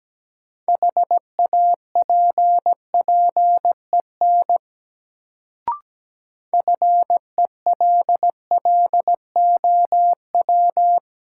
Morse code, heard as a keyed tone, about 17 wpm, 700 Hz. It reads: HAPPEN FELLOW
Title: Sets of 5 Words Encore - New Word Drill - 17wpm